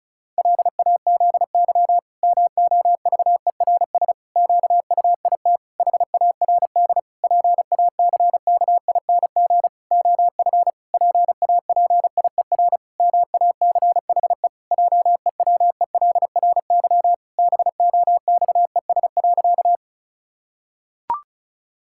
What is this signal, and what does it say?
Morse code, heard as a keyed tone, about 36 wpm, 700 Hz. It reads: LAZY MOVERS QUIT HARD PACKING OF PAPIER MACHE JEWELRY BOXES.
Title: Pangram Challenge v2 - 35wpm